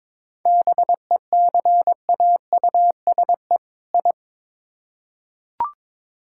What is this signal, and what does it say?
Morse code, heard as a keyed tone, about 22 wpm, 700 Hz. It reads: BECAUSE I
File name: Build A Phrase - 2-Word Phrases - 22wpm